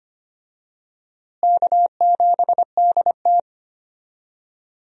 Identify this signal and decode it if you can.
Morse code, heard as a keyed tone, about 25 wpm, 700 Hz. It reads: K7DT
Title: USA Call Signs - 4 Characters - Repeated 2x - 25wpm